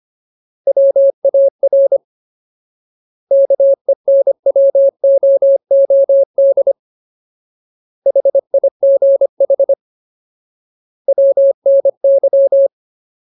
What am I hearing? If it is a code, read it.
Morse code, heard as a keyed tone, about 25 wpm, 550 Hz. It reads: WAR KENWOOD HIGH WNY
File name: Ultimate Word List / 25WPM Ultimate